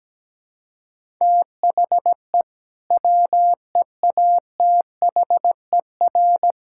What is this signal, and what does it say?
Morse code, heard as a keyed tone, about 17 wpm, 700 Hz. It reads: THE WEATHER
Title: Sentences from Top 1000 Words - Review Part 5 - 17wpm